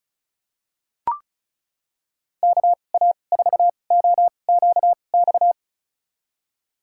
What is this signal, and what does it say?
Morse code, heard as a keyed tone, about 34 wpm, spaced as for 25 wpm, 700 Hz. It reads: KA4OQX
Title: USA Call Signs - 6 Characters - 2x Character Spacing - 35wpm